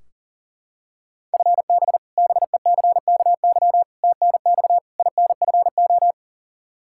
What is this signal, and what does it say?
Morse code, heard as a keyed tone, about 40 wpm, 700 Hz. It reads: FB BECKY TNX INFO
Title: Practice QSO - Exchange 3 - More Info - 40wpm